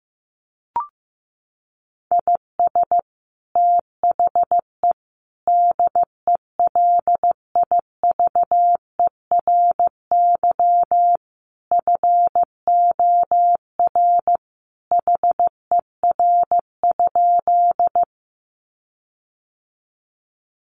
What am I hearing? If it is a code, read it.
Morse code, heard as a keyed tone, about 15 wpm, 700 Hz. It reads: IS THE DELIVERY FOR HER?
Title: Sentences from Top 1500 Words 15wpm